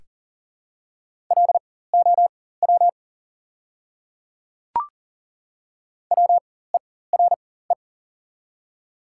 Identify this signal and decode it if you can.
Morse code, heard as a keyed tone, about 40 wpm, spaced as for 19 wpm, 690 Hz. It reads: LOW WERE
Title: 4x Character Spacing - Top 200 Words - 40wpm